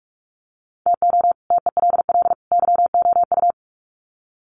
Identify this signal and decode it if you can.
Morse code, heard as a keyed tone, about 44 wpm, 700 Hz. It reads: TO TELL YOU
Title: Sets of 3 Words 45wpm